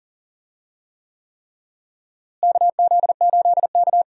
Morse code, heard as a keyed tone, about 40 wpm, 700 Hz. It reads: KZ8K
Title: USA Call Signs - 4 Characters - Repeated 3x - 40wpm